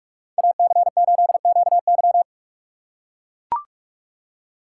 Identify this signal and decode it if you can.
Morse code, heard as a keyed tone, about 42 wpm, 700 Hz. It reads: AC8QY
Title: USA Call Signs - 2x Character Spacing - 45wpm